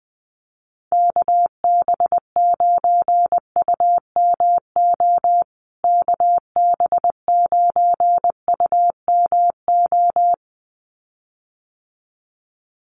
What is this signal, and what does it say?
Morse code, heard as a keyed tone, about 20 wpm, 700 Hz. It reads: KB9UMO KB9UMO
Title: USA Call Signs Repeated 2x 20wpm